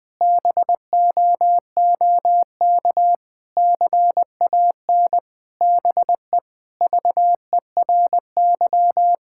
Morse code, heard as a keyed tone, about 20 wpm, 700 Hz. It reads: BOOK CAN BE VERY
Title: Sentences with 2-Word Phrases - Spoken Beginning - 20wpm